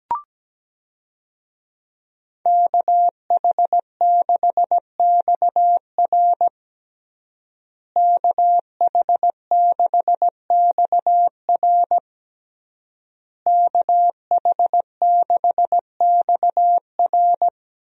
Morse code, heard as a keyed tone, about 17 wpm, 700 Hz. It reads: KH6XR KH6XR KH6XR
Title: DXCC-100 Call Signs - 5 Characters - Repeated 3x - 17wpm